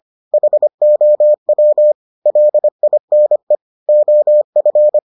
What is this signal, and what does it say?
Morse code, heard as a keyed tone, about 25 wpm, 600 Hz. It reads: HOW LINE OF